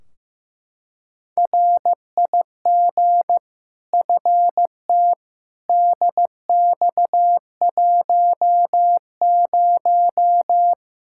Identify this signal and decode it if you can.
Morse code, heard as a keyed tone, about 15 wpm, 700 Hz. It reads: RIG FT DX10
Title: K1USN SST QSO Practice 15wpm